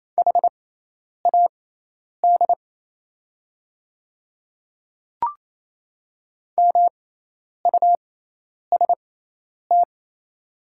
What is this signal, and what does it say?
Morse code, heard as a keyed tone, about 28 wpm, spaced as for 10 wpm, 690 Hz. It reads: HAD MUST
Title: 6x Character Spacing - Top 300 Words - 28wpm